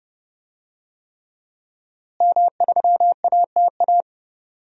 Morse code, heard as a keyed tone, about 31 wpm, 700 Hz. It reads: M3ATA
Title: DXCC-100 Call Signs - 5 Characters - Repeated 3x - 30wpm